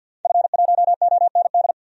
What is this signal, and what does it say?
Morse code, heard as a keyed tone, about 52 wpm, 700 Hz. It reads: F1OND
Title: DXCC-100 Call Signs - 5 Characters - Repeated 2x - 50wpm